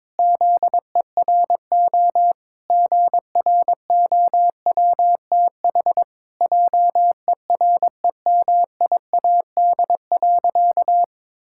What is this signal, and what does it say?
Morse code, heard as a keyed tone, about 21 wpm, 700 Hz. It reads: ZERO GROWTH JEREMIAD.
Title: Pangram Challenge v2 - 22wpm